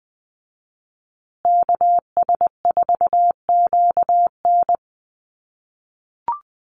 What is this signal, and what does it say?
Morse code, heard as a keyed tone, about 20 wpm, 700 Hz. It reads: KS4QN
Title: USA Call Signs Repeated 3x 20wpm